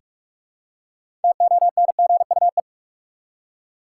Morse code, heard as a keyed tone, about 46 wpm, 700 Hz. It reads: TONGUE
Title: Six Letter Words - 45wpm